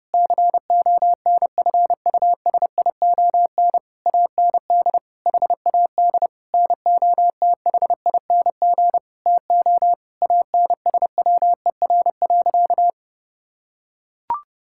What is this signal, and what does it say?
Morse code, heard as a keyed tone, about 30 wpm, 700 Hz. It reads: CONFUSION AND HAD NOTHING TO ANSWER.
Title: Sentences from Top 1600 Words 30wpm